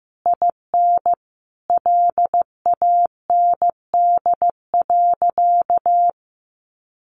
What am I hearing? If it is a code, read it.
Morse code, heard as a keyed tone, about 15 wpm, 700 Hz. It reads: IN LAND.
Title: Sentences from Top 1100 Words 15wpm